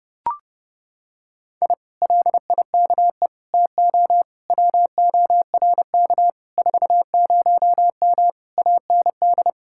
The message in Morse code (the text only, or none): I LIKE TO WORK 40M AND